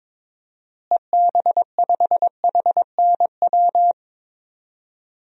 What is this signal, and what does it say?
Morse code, heard as a keyed tone, about 22 wpm, 700 Hz. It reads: EB5HNW
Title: DXCC-100 Call Signs - 6 Characters - Repeated 2x - 22wpm